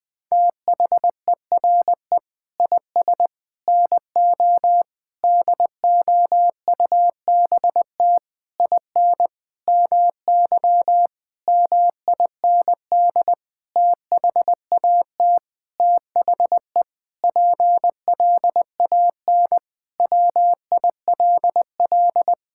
Morse code, heard as a keyed tone, about 21 wpm, 700 Hz. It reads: THERE IS NO DOUBT IN MY MIND THAT THE PLAN WILL
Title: Sentences with 9-Word Phrases - 20wpm